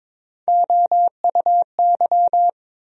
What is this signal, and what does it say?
Morse code, heard as a keyed tone, about 22 wpm, 700 Hz. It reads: OUY
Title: USA Call Sign Postfixes - 3 Letters 22wpm